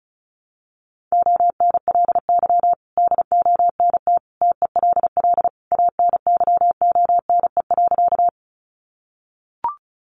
Morse code, heard as a keyed tone, about 34 wpm, 700 Hz. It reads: ONLY DONT TELL ANYONE.
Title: Sentences from Top 900 Words 35wpm